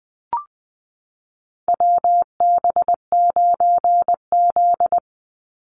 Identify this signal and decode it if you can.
Morse code, heard as a keyed tone, about 20 wpm, 700 Hz. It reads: WB9Z
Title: Call Signs Repeated 2x - 20wpm